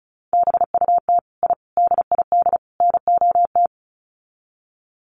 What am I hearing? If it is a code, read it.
Morse code, heard as a keyed tone, about 34 wpm, 700 Hz. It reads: BUT I DID NOT
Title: Sets of 4 Words 35wpm